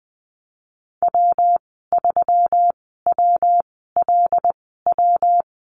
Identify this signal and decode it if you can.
Morse code, heard as a keyed tone, about 20 wpm, spaced as for 15 wpm, 700 Hz. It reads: W3WLW
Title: 154 Lesson 25g - Letter p - Call Signs 20wpm